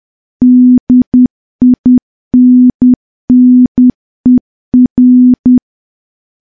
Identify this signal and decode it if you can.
Morse code, heard as a keyed tone, about 10 wpm, 250 Hz. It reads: DINNER
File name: morse / morse-training-10WPM